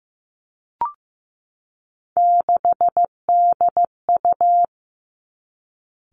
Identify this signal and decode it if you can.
Morse code, heard as a keyed tone, about 15 wpm, 700 Hz. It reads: 6DU
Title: Three Challenging Letter-Number Combinations 15wpm